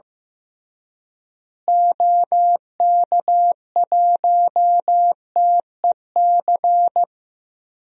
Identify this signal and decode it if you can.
Morse code, heard as a keyed tone, about 15 wpm, 700 Hz. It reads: OK1TEC
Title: DXCC-100 Call Signs - 6 Characters - Repeated 3x - 15wpm